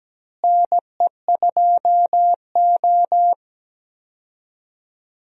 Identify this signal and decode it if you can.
Morse code, heard as a keyed tone, about 17 wpm, 700 Hz. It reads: NE2O
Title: USA Call Signs - 4 Characters - Repeated 3x - 17wpm